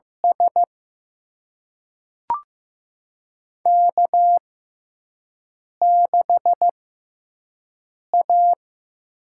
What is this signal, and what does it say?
Morse code, heard as a keyed tone, about 15 wpm, spaced as for 5 wpm, 700 Hz. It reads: S K6A
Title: USA Call Signs - 4 Characters - 6x Character Spacing - 15wpm